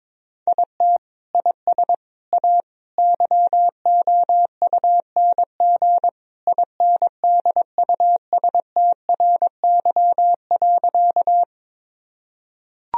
Morse code, heard as a keyed tone, about 21 wpm, 700 Hz. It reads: IT IS A YOUNG INDUSTRY.
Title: Sentences from Top 1000 Words - Review Part 4 - 22wpm